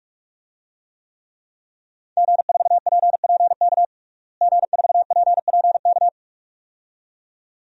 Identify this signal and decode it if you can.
Morse code, heard as a keyed tone, about 46 wpm, 700 Hz. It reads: G4PPK G4PPK
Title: DXCC-100 Call Signs - 5 Characters - Repeated 2x - 45wpm